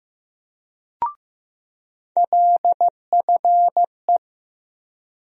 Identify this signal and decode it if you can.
Morse code, heard as a keyed tone, about 15 wpm, 700 Hz. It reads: LFE
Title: USA Call Sign Postfixes - 3 Letters 15wpm